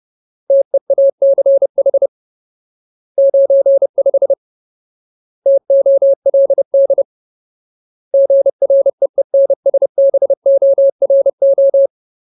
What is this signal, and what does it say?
Morse code, heard as a keyed tone, about 30 wpm, 550 Hz. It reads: TEACH 95 TOLD GREENSBORO
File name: Ultimate Word List / 30WPM Ultimate